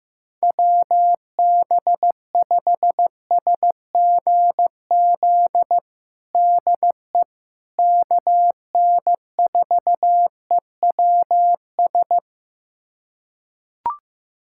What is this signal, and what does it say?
Morse code, heard as a keyed tone, about 15 wpm, 700 Hz. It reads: WB5SGZ DE KN4EWS
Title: USA Callsign DE Callsign 15wpm